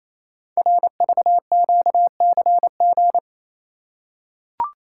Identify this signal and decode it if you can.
Morse code, heard as a keyed tone, about 28 wpm, 700 Hz. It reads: RVQCG
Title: Code Groups - 5 Letters - 28wpm